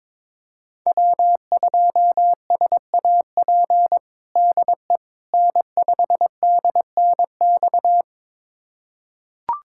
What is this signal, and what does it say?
Morse code, heard as a keyed tone, about 22 wpm, 700 Hz. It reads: W2SAP DE N5DNX
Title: USA Callsign DE Callsign - 5 Characters - 22wpm